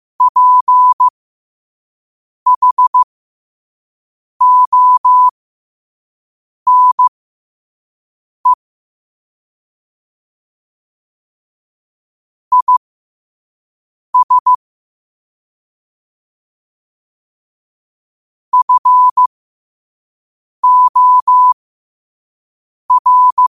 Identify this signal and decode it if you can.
Morse code, heard as a keyed tone, about 15 wpm, spaced as for 5 wpm, 1000 Hz. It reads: PHONE IS FOR
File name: qotdfiles / 5wpm